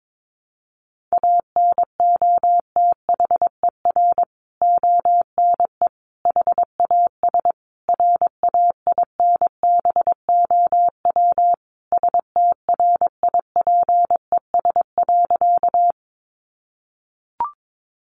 Morse code, heard as a keyed tone, about 22 wpm, 700 Hz. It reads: ANOTHER ONE HAS RAINBOW STRIPES.
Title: Additional Easy Phrases 22wpm